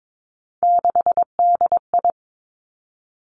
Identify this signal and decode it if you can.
Morse code, heard as a keyed tone, about 22 wpm, 700 Hz. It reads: 6DI
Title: Three Challenging Letter-Number Combinations 22wpm